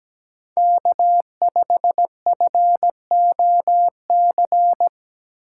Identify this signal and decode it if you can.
Morse code, heard as a keyed tone, about 17 wpm, 700 Hz. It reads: K5FOC
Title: USA Call Signs - 5 Characters - 6x Character Spacing - 17wpm